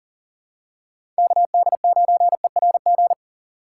Morse code, heard as a keyed tone, about 40 wpm, 700 Hz. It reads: KD9ERG
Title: USA Call Signs - 6 Characters - 3x Character Spacing - 40wpm